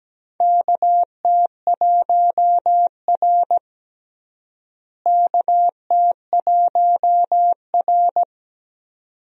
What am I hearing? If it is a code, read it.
Morse code, heard as a keyed tone, about 17 wpm, 700 Hz. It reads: KT1R KT1R
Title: USA Call Signs - 4 Characters - Repeated 2x - 17wpm